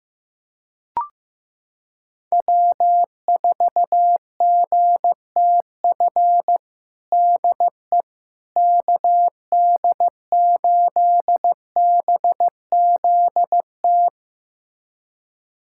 Morse code, heard as a keyed tone, about 15 wpm, 700 Hz. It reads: W4GTF DE KD8BZT
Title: USA Callsign DE Callsign 15wpm